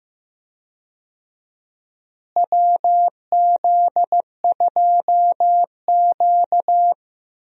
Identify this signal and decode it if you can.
Morse code, heard as a keyed tone, about 15 wpm, 700 Hz. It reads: WZ2Q
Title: USA Call Signs - 4 Characters - Repeated 2x - 15wpm